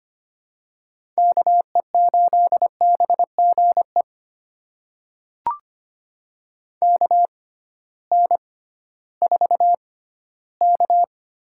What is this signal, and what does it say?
Morse code, heard as a keyed tone, about 25 wpm, spaced as for 15 wpm, 700 Hz. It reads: KE8BGE KN4K
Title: USA Call Signs - 6 Characters - 6x Character Spacing - 25wpm